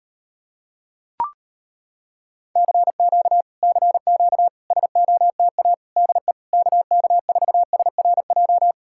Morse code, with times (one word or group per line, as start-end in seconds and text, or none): 2.6-3.4 CQ
3.6-4.5 CQ
4.7-5.7 SOTA
6.0-6.3 DE
6.5-8.7 KK4SRJ